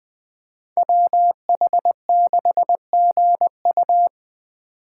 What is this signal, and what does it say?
Morse code, heard as a keyed tone, about 21 wpm, 700 Hz. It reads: WH6GU
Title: USA Call Signs - 5 Characters - 2x Character Spacing - 20wpm